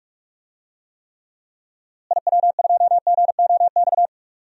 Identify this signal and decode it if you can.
Morse code, heard as a keyed tone, about 47 wpm, 700 Hz. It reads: IW2GOX
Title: DXCC-100 Call Signs - 6 Characters - Repeated 2x - 45wpm